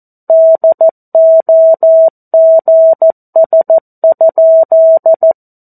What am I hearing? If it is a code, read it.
Morse code, heard as a keyed tone, about 14 wpm, 650 Hz. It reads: DOGS?
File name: 02-Chapter-02 14 WPM